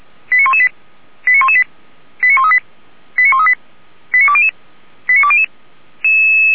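Five-tone selective call (ZVEI), heard as quick runs of five tones, tones of about 70 ms. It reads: 89118 89118 89138 89138 89200 89200
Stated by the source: ZVEI five-tone selective call received by SDR.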